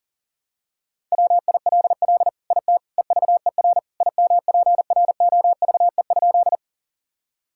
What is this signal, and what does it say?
Morse code, heard as a keyed tone, about 37 wpm, 690 Hz. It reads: WILL IT EVER IMPROVE?
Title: Sentences from Top 1000 Words - Review Part 5 - 40wpm